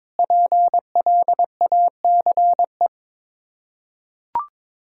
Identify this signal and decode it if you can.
Morse code, heard as a keyed tone, about 22 wpm, 700 Hz. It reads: PLACE
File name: practice / 22wpm / Sets of 7 Words - 22wpm